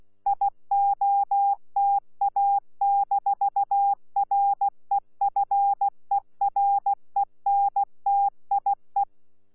Morse code, heard as a keyed tone, about 16 wpm, 800 Hz. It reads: IOTA-REFERENTIE
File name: Z PI4VRZ / PI4VRZ 14-jun-2025